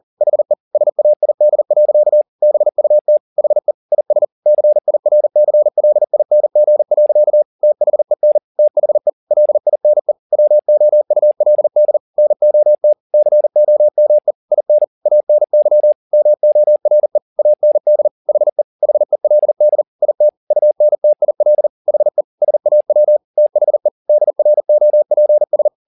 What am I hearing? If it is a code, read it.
Morse code, heard as a keyed tone, about 40 wpm, 600 Hz. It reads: HE SAID. BUT HE IS CIRCLING. THEN THE LINE WOULD NOT COME IN ANY MORE AND HE HELD IT UNTIL HE SAW THE DROPS
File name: the old man and the sea 2 40wpm